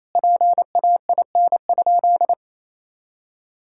Morse code, heard as a keyed tone, about 28 wpm, 700 Hz. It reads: PAIN?
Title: Sentences from Top 1600 Words 28wpm